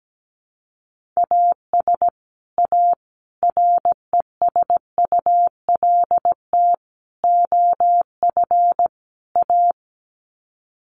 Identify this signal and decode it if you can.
Morse code, heard as a keyed tone, about 17 wpm, 700 Hz. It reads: AS A RESULT OF A
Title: Sets of 5 Words Encore 17wpm